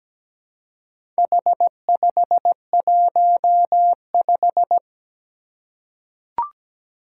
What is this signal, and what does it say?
Morse code, heard as a keyed tone, about 17 wpm, 700 Hz. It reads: H515
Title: Four Challenging Letter-Number Combinations 17wpm